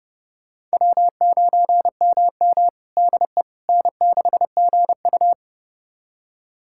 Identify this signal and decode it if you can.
Morse code, heard as a keyed tone, about 30 wpm, 700 Hz. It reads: W9MM DE N6GU
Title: USA Callsign DE Callsign - 4 Characters - 30wpm